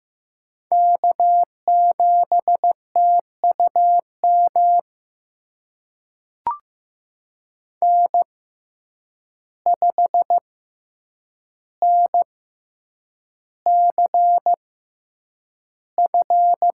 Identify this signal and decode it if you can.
Morse code, heard as a keyed tone, about 15 wpm, spaced as for 8 wpm, 700 Hz. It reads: K7TUM N5NCF
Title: USA Call Signs - 5 Characters - 6x Character Spacing - 15wpm